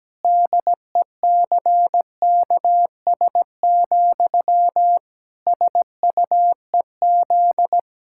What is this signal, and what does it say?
Morse code, heard as a keyed tone, about 17 wpm, 700 Hz. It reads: DECKS, SUEZ
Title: Pangram Challenge v2 - 17wpm